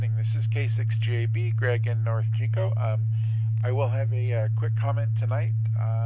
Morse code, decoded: none

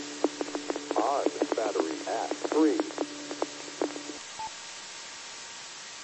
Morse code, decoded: none

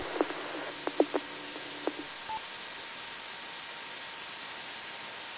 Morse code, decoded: none

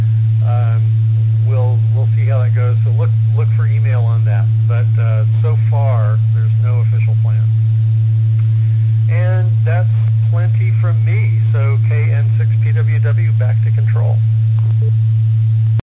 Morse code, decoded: none

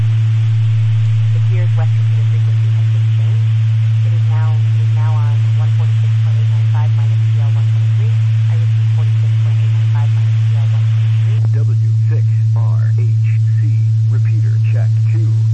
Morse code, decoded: none

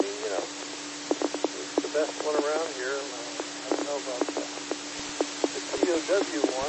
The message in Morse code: none

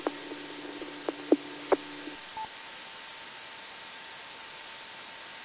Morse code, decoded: none

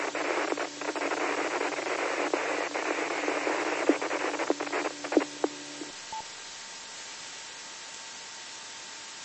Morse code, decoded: none

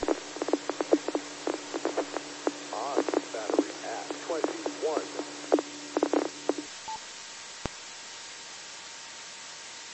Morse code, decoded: none

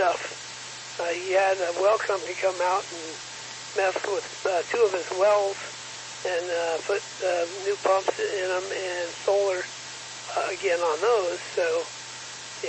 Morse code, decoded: none

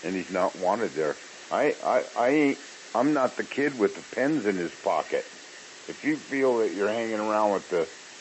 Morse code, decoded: none

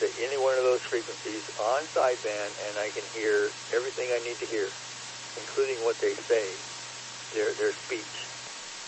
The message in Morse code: none

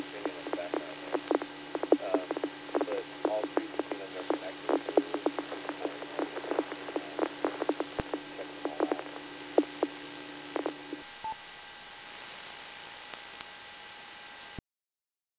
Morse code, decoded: none

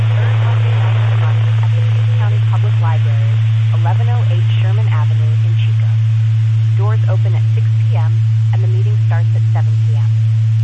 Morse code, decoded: none

